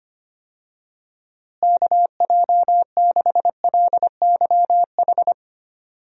Morse code, heard as a keyed tone, about 25 wpm, 700 Hz. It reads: KJ6LYH
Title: USA Call Signs - 6 Characters - Repeated 2x - 25wpm